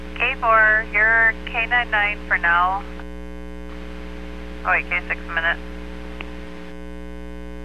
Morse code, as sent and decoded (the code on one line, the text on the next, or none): none